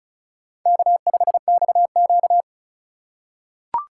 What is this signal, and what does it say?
Morse code, heard as a keyed tone, about 35 wpm, 700 Hz. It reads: K5XQ
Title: K1USN SST QSO Practice - 35wpm